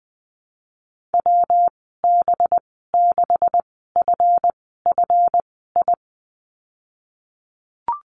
Morse code, heard as a keyed tone, about 20 wpm, spaced as for 15 wpm, 700 Hz. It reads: WB6FFI
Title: 193 Lesson 31g - Letter v - Call Signs 20wpm